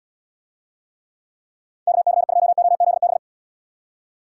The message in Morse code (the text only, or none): KC2OZB